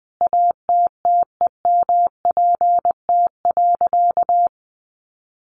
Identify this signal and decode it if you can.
Morse code, heard as a keyed tone, about 20 wpm, 700 Hz. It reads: ATTEMPT.
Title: Sentences from Top 1000 Words 20wpm